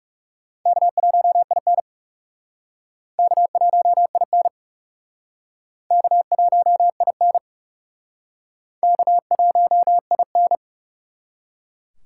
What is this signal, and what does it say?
Morse code, heard as a keyed tone, about 38 wpm, 700 Hz. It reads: K1IN K1IN K1IN K1IN